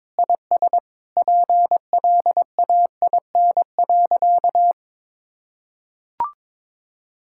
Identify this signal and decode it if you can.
Morse code, heard as a keyed tone, about 22 wpm, 700 Hz. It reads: IS PLAIN.
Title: Sentences from Top 1000 Words - Review Part 5 - 22wpm